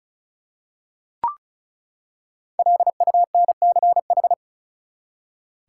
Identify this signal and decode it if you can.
Morse code, heard as a keyed tone, about 36 wpm, 700 Hz. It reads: LUNCH
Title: Five Letter Words - 35wpm